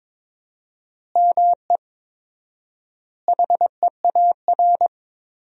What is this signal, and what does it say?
Morse code, heard as a keyed tone, about 22 wpm, 700 Hz. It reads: ME HEAR
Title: 4x Word-Spacing - Sentences from Top 400 Words - 22wpm